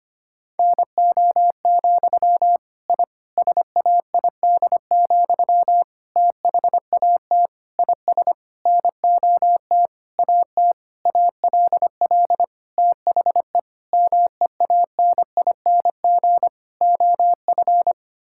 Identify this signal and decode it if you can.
Morse code, heard as a keyed tone, about 25 wpm, 700 Hz. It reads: NO, I SAID, THAT IS NOT AT ALL THE MEANING OF
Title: Sentences from Top 1000 Words - Review Part 6 - 25wpm